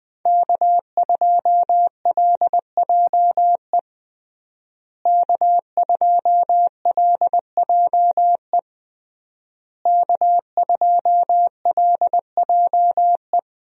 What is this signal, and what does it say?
Morse code, heard as a keyed tone, about 20 wpm, 700 Hz. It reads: K2LJE K2LJE K2LJE
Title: USA Call Signs - 5 Characters - Repeated 3x - 20wpm